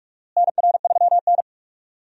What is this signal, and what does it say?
Morse code, heard as a keyed tone, about 46 wpm, 700 Hz. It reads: NR3N
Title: USA Call Signs - 4 Characters - Repeated 2x - 45wpm